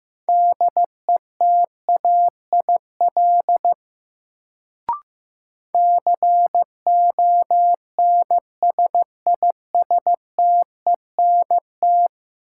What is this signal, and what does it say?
Morse code, heard as a keyed tone, about 15 wpm, 700 Hz. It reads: DETAIL CONSISTENT
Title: Sentences from Top 1500 Words - New Word Drill - 15wpm